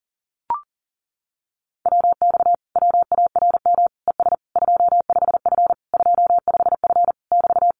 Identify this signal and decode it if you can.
Morse code, heard as a keyed tone, about 40 wpm, 700 Hz. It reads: WX WARM ES 25F 25F <BT>
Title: Common QSO Phrases Encore 40wpm